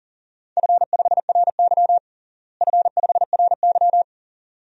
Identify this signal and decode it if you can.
Morse code, heard as a keyed tone, about 37 wpm, 700 Hz. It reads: F5RY F5RY
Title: DXCC-100 Call Signs - 4 Characters - Repeated 2x - 40wpm